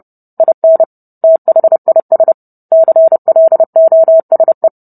Morse code, heard as a keyed tone, about 30 wpm, 650 Hz. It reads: IN THIS CLOSE